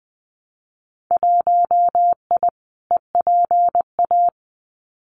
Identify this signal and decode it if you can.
Morse code, heard as a keyed tone, about 20 wpm, 700 Hz. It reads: 1I EPA
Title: Winter Field Day Exchanges Encore 20wpm